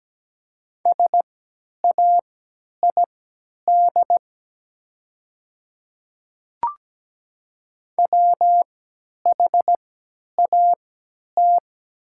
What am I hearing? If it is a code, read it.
Morse code, heard as a keyed tone, about 17 wpm, spaced as for 10 wpm, 700 Hz. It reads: SAID WHAT
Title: 3x Character Spacing - Top 100 Words - 17wpm